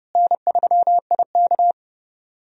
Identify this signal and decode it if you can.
Morse code, heard as a keyed tone, about 30 wpm, 700 Hz. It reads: N3IK
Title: USA Call Signs - 4 Characters - Repeated 3x - 30wpm